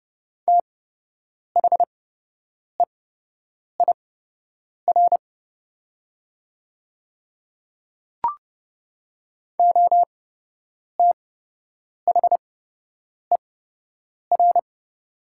Morse code, heard as a keyed tone, about 30 wpm, spaced as for 8 wpm, 700 Hz. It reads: THEIR OTHER
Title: 8x Character Spacing - Top 100 Words - 30wpm